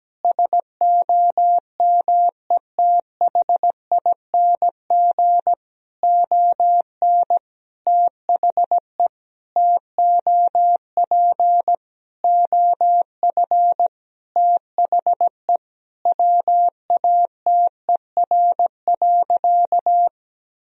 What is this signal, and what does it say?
Morse code, heard as a keyed tone, about 17 wpm, 700 Hz. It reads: SOMETHING ON THE TOP OF THE WATER.
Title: Sentences from Top 1000 Words - Review Part 2 - 17wpm